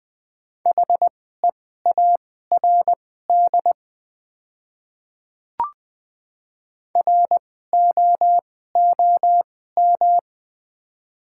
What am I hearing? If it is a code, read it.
Morse code, heard as a keyed tone, about 20 wpm, spaced as for 14 wpm, 700 Hz. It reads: HEARD ROOM
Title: 2x Character Spacing - Top 500 Words - 20wpm